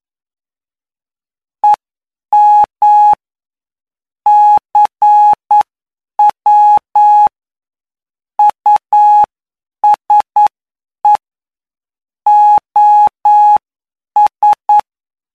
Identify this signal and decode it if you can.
Morse code, heard as a keyed tone, about 9 wpm, spaced as for 7 wpm, 840 Hz. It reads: EM CW USE OS
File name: cw / py2lcd / 20-PY2LCD-Li��o 29